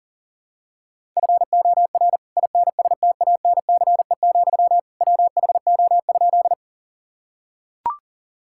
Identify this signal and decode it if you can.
Morse code, heard as a keyed tone, about 40 wpm, 700 Hz. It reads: FOR INSTANCE, WHO?
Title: Sentences from Top 1000 Words - Review Part 5 - 40wpm